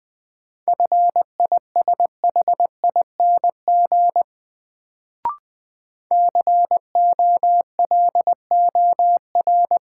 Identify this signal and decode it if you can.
Morse code, heard as a keyed tone, about 20 wpm, 700 Hz. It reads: FISHING COLOR
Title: Sentences from Top 1600 Words - New Word Drill Alt - 20wpm